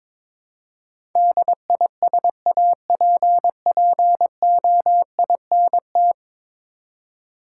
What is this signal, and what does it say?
Morse code, heard as a keyed tone, about 22 wpm, 700 Hz. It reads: DISAPPOINT
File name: practice / 22wpm / 10 Letter Words - First 3 Letters Given - 22wpm